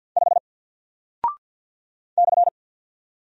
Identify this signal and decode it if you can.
Morse code, heard as a keyed tone, about 49 wpm, 700 Hz. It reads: 5 /
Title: Single Character (Letter:Number:Punctuation:Prosign) Alt 50wpm